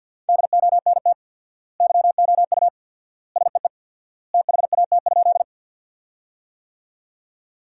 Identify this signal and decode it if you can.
Morse code, heard as a keyed tone, about 47 wpm, 700 Hz. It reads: DONT YOU SEE THAT?
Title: 4x Word-Spacing - Sentences from Top 300 Words - 50wpm